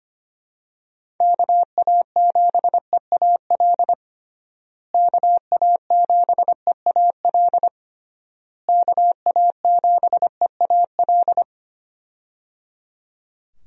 Morse code, heard as a keyed tone, about 25 wpm, 700 Hz. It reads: KA7EAL KA7EAL KA7EAL
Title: USA Call Signs - 6 Characters - Repeated 3x - 25wpm